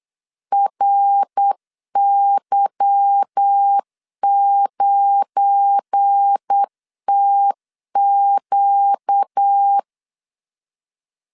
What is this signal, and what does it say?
Morse code, heard as a keyed tone, about 8 wpm, 800 Hz. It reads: RY9TQ
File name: LEZIONI / 8 YJ45 / 8 Y 8wpm 3245